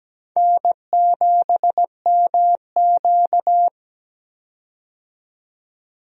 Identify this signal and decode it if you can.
Morse code, heard as a keyed tone, about 17 wpm, 700 Hz. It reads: N7MQ
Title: USA Call Signs - 4 Characters - 17wpm